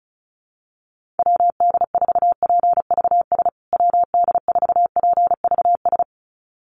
Morse code, heard as a keyed tone, about 34 wpm, 700 Hz. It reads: WD4PVS WD4PVS
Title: USA Call Signs Repeated 2x 35wpm